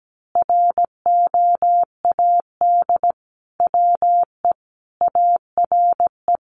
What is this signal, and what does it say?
Morse code, heard as a keyed tone, about 17 wpm, 700 Hz. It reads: ROAD WE ARE
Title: Sentences from Top 1100 Words 17wpm